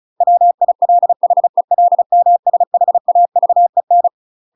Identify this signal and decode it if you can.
Morse code, heard as a keyed tone, about 34 wpm, 700 Hz. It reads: WILHELMSHAVEN
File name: UK MASTER QSOs TRAINING / 35WPM / UK MASTER QSOs 0015